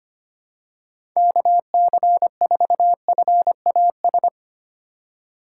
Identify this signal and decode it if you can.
Morse code, heard as a keyed tone, about 25 wpm, 700 Hz. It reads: KC4FAS
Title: USA Call Signs - 4x Character Spacing - 25wpm